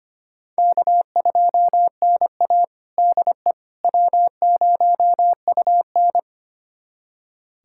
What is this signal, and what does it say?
Morse code, heard as a keyed tone, about 25 wpm, 700 Hz. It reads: K2NA DE W0UN
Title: USA Callsign DE Callsign - 4 Characters - 25wpm